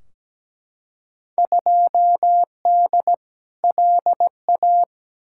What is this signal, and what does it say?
Morse code, heard as a keyed tone, about 17 wpm, 700 Hz. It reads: 2D LA
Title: Field Day Exchanges - 17wpm